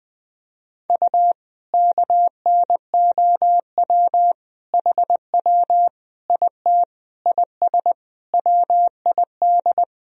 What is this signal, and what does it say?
Morse code, heard as a keyed tone, about 20 wpm, 700 Hz. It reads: U KNOW HW IT IS WID
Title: Sentences from Top 100 Words and Abbreviations 20wpm